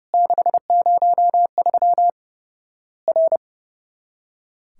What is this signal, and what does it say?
Morse code, heard as a keyed tone, about 30 wpm, 700 Hz. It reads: 603 R
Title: Worked All Europe DX Contest - QTC Exchange - 30wpm